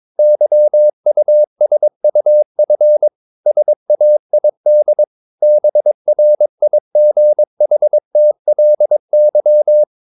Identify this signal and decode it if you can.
Morse code, heard as a keyed tone, about 22 wpm, 600 Hz. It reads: YUSUF SAID BRIGHTLY
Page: Romanen Never af Ken Follet på engelsk.